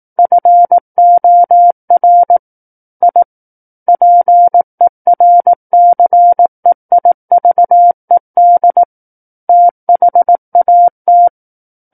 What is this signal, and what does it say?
Morse code, heard as a keyed tone, about 18 wpm, 700 Hz. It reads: FOR I PERCEIVED THAT
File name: 18 WPM